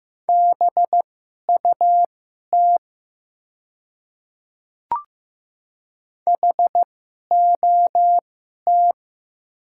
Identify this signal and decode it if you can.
Morse code, heard as a keyed tone, about 15 wpm, spaced as for 11 wpm, 700 Hz. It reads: BUT HOT
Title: 2x Character Spacing - Top 100 Words - 15wpm